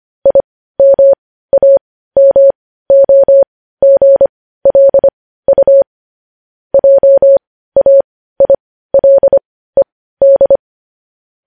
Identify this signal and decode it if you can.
Morse code, heard as a keyed tone, about 25 wpm, spaced as for 15 wpm, 550 Hz. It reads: IMAMOGLU JAILED
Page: news-headlines-in-morse-code-at-15-wpm